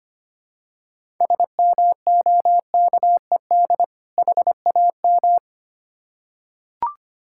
Morse code, heard as a keyed tone, about 25 wpm, 700 Hz. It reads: SMOKED HAM
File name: practice / 25wpm / Difficult Words with Hints - 25wpm